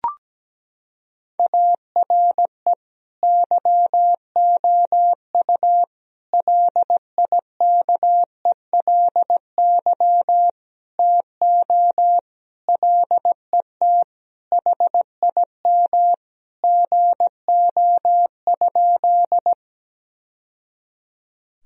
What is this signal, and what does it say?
Morse code, heard as a keyed tone, about 17 wpm, 700 Hz. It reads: ARE YOU LIKELY TO LET HIM GO?
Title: Sentences from Top 1000 Words - Review Part 4 - 17wpm